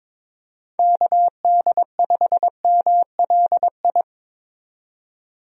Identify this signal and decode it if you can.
Morse code, heard as a keyed tone, about 22 wpm, 700 Hz. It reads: KD5MLI
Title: USA Call Signs - 6 Characters - 22wpm